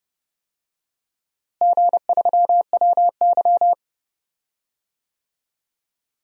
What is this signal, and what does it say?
Morse code, heard as a keyed tone, about 32 wpm, 700 Hz. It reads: G3WY
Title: DXCC-100 Call Signs - 4 Characters - Repeated 3x - 30wpm